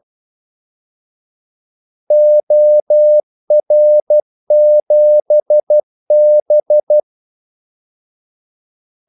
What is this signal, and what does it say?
Morse code, heard as a keyed tone, about 12 wpm, 600 Hz. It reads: OR7B
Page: 2 ekstra ordmellemrum mellem kaldesignalerne